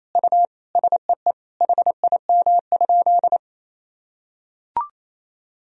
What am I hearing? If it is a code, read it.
Morse code, heard as a keyed tone, about 28 wpm, 700 Hz. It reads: U SEE HIM?
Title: Sentences from Top 100 Words and Abbreviations 28wpm